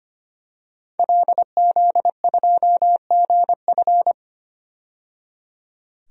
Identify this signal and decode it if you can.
Morse code, heard as a keyed tone, about 25 wpm, 700 Hz. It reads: LZ2GF
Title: DXCC-100 Call Signs - 5 Characters - 25wpm